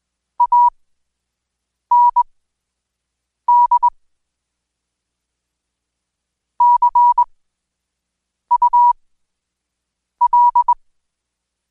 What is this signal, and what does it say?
Morse code, heard as a keyed tone, about 21 wpm, spaced as for 6 wpm, 970 Hz. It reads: AND CUL
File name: Code Quick/AudioDownld2/Seg109-Day14-DayDrill 14